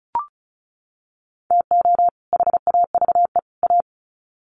Morse code, heard as a keyed tone, about 34 wpm, 700 Hz. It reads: TO HAVE A
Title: Sets of 3 Words 35wpm